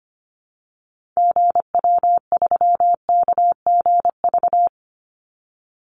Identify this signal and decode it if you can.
Morse code, heard as a keyed tone, about 25 wpm, 700 Hz. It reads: GW3KGV
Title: Call Signs Repeated 3x - 25wpm